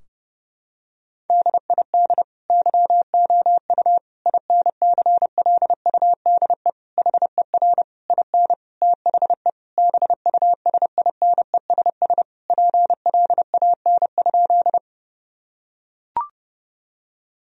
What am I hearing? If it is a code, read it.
Morse code, heard as a keyed tone, about 30 wpm, 700 Hz. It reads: DID YOU INCLUDE HER IN THE BUSINESS PLAN?
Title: Sentences from Top 1000 Words - Review Part 3 - 30wpm